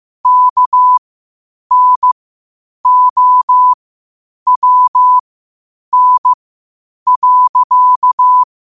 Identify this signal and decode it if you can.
Morse code, heard as a keyed tone, about 15 wpm, spaced as for 8 wpm, 1000 Hz. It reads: KNOWN.